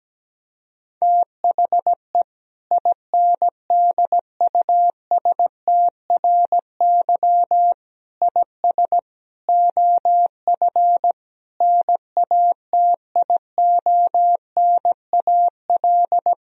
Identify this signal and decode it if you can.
Morse code, heard as a keyed tone, about 17 wpm, 700 Hz. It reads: THE INDUSTRY IS OF NATIONAL
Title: Sentences from Top 1000 Words - Review Part 4 - 17wpm